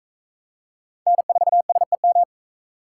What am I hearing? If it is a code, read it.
Morse code, heard as a keyed tone, about 38 wpm, 700 Hz. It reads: N4SEM